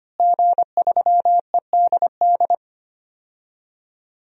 Morse code, heard as a keyed tone, about 25 wpm, 700 Hz. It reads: G3EDD
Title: DXCC-100 Call Signs - 5 Characters - Repeated 3x - 25wpm